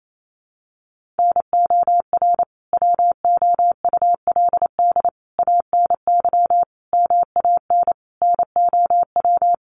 Morse code, heard as a keyed tone, about 28 wpm, 700 Hz. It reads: NOR WOULD ANY MAN NOW
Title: Sentences from Top 1500 Words 28wpm